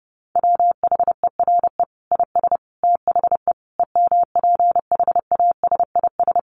Morse code, heard as a keyed tone, about 30 wpm, 700 Hz. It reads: WHERE IS THE EMPHASIS
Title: Sentences from Top 1400 Words 30wpm